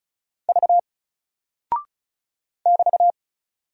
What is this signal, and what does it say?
Morse code, heard as a keyed tone, about 36 wpm, 700 Hz. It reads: V <BT>
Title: Single Character (Letter:Number:Punctuation:Prosign) Alt 35wpm